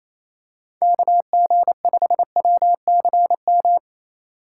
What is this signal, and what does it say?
Morse code, heard as a keyed tone, about 28 wpm, 700 Hz. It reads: KG5WCM
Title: USA Call Signs - 6 Characters - 8x Character Spacing - 28wpm